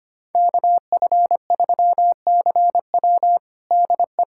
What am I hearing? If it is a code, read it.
Morse code, heard as a keyed tone, about 25 wpm, 700 Hz. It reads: KF3CW DE
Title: USA Callsign DE Callsign - 5 Characters - 25wpm